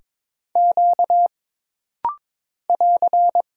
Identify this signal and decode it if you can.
Morse code, heard as a keyed tone, about 22 wpm, 700 Hz. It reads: Q <AR>
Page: This rapid-fire timing is geared to push your ICR (Instant Character Recognition) skills.